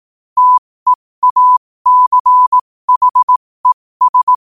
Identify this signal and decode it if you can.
Morse code, heard as a keyed tone, about 18 wpm, spaced as for 15 wpm, 1000 Hz. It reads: TEACHES
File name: qotdfiles / 15wpm